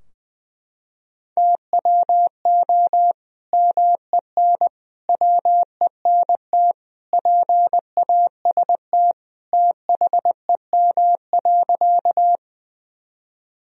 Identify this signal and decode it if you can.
Morse code, heard as a keyed tone, about 20 wpm, 700 Hz. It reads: TWO MEN WENT PAST THEM.
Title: Sentences from Top 1000 Words - Review Part 3 - 20wpm